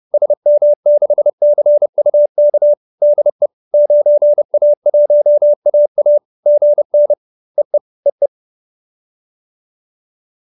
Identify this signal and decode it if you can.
Morse code, heard as a keyed tone, about 30 wpm, 590 Hz. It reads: SM6CUK DE 9A1AA GN EE EE